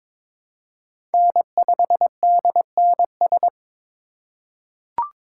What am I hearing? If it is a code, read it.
Morse code, heard as a keyed tone, about 23 wpm, 700 Hz. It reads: N5DNS
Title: USA Call Signs - 5 Characters - 4x Character Spacing - 22wpm